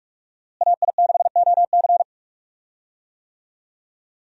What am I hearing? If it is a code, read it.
Morse code, heard as a keyed tone, about 44 wpm, 700 Hz. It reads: AI6OC
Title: USA Call Signs - 5 Characters - Repeated 2x - 45wpm